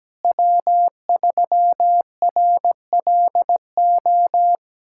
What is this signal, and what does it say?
Morse code, heard as a keyed tone, about 17 wpm, 700 Hz. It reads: W3RLO
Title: K1USN SST QSO Practice - 17wpm